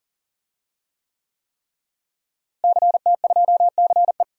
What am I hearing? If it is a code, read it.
Morse code, heard as a keyed tone, about 40 wpm, 700 Hz. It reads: CT2CE
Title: DXCC-100 Call Signs - 5 Characters - Repeated 3x - 40wpm